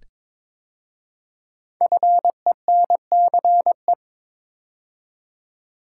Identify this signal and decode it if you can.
Morse code, heard as a keyed tone, about 22 wpm, 700 Hz. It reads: FENCE
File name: 5 Letter Words - First 3 Letters Given - 22wpm